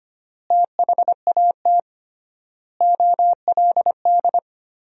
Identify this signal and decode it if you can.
Morse code, heard as a keyed tone, about 25 wpm, 700 Hz. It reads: THAT OLD
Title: 3x Word-Spacing - Sentences from Top 400 Words - 25wpm